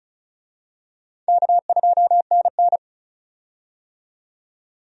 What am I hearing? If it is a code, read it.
Morse code, heard as a keyed tone, about 35 wpm, 700 Hz. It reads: K2NN
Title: USA Call Signs - 4 Characters - 35wpm